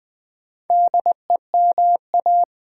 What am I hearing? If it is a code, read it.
Morse code, heard as a keyed tone, about 20 wpm, 700 Hz. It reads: DEMA
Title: Build A Word - 2 Letters At A Time - 6-Letter Words - 20wpm